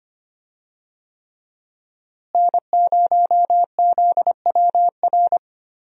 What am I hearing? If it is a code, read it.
Morse code, heard as a keyed tone, about 25 wpm, 700 Hz. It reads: N0ZWR
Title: USA Call Signs - 5 Characters - Repeated 3x - 25wpm